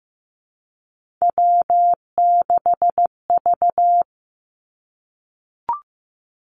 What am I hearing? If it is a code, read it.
Morse code, heard as a keyed tone, about 15 wpm, 700 Hz. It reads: W6V
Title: Three Challenging Letter-Number Combinations 15wpm